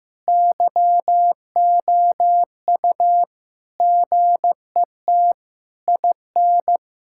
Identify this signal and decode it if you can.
Morse code, heard as a keyed tone, about 15 wpm, 700 Hz. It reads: YOU GET IN
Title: Sentences from Top 1000 Words - Review Part 6 - 15wpm